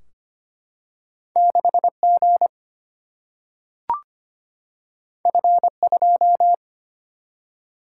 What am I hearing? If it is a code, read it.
Morse code, heard as a keyed tone, about 25 wpm, 700 Hz. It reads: 6G F2
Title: Single-Letter Single-Number 25wpm